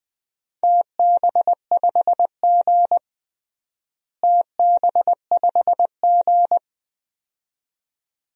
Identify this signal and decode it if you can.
Morse code, heard as a keyed tone, about 20 wpm, 700 Hz. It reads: TB5G TB5G
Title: DXCC-100 Call Signs - 4 Characters - Repeated 2x - 20wpm